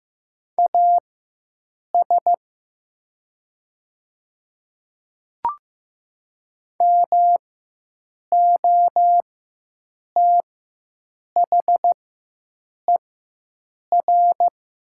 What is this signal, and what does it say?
Morse code, heard as a keyed tone, about 15 wpm, spaced as for 7 wpm, 700 Hz. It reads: AS MOTHER
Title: 4x Character Spacing - Top 200 Words - 15wpm